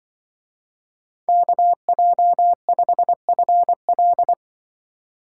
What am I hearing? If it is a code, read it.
Morse code, heard as a keyed tone, about 23 wpm, 700 Hz. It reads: KJ5FL
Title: Speed Racing - USA 2x2 Call Signs 20wpm